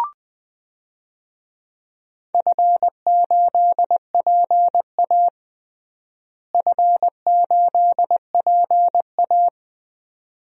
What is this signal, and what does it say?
Morse code, heard as a keyed tone, about 21 wpm, 700 Hz. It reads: F8PA F8PA
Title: DXCC-100 Call Signs - 4 Characters - Repeated 2x - 20wpm